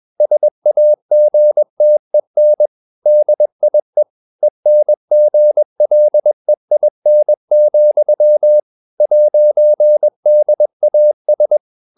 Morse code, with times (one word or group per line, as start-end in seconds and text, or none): 0.2-2.7 SAGTEN
3.1-4.0 DIE
4.4-8.6 ENGLEIN,
9.0-11.6 'DAS